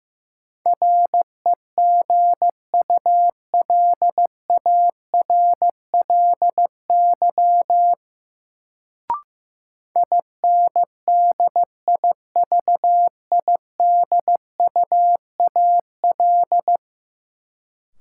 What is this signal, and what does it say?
Morse code, heard as a keyed tone, about 15 wpm, 700 Hz. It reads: REGULARLY INDIVIDUAL
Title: Sentences from Top 1400 Words - New Word Drill Alt - 15wpm